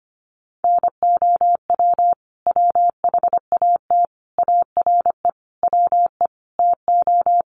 Morse code, heard as a keyed tone, about 25 wpm, 700 Hz. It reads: NOW WHAT ARE WE TO
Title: Sentences from Top 100 Words 25wpm